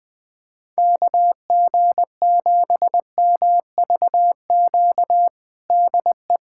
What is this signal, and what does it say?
Morse code, heard as a keyed tone, about 21 wpm, 700 Hz. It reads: KG7MVQ DE
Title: USA Callsign DE Callsign 20wpm